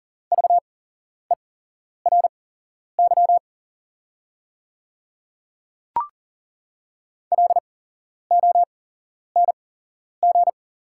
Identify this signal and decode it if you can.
Morse code, heard as a keyed tone, about 40 wpm, spaced as for 11 wpm, 690 Hz. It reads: VERY LONG
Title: 8x Character Spacing - Top 300 Words - 40wpm